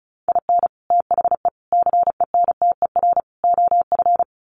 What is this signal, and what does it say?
Morse code, heard as a keyed tone, about 34 wpm, 700 Hz. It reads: IN THE CENTER OF
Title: Sets of 4 Words 35wpm